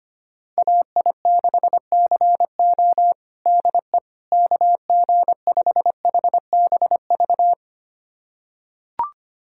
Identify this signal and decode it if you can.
Morse code, heard as a keyed tone, about 25 wpm, 700 Hz. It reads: AI6CO DE KG5HBV
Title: USA Callsign DE Callsign 25wpm